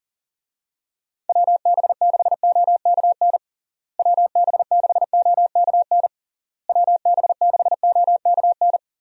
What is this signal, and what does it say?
Morse code, heard as a keyed tone, about 40 wpm, 700 Hz. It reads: WB6OKN WB6OKN WB6OKN
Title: USA Call Signs - 6 Characters - Repeated 3x - 40wpm